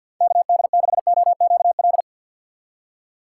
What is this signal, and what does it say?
Morse code, heard as a keyed tone, about 53 wpm, 700 Hz. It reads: KD6OQL
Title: USA Call Signs - 6 Characters - 8x Character Spacing - 50wpm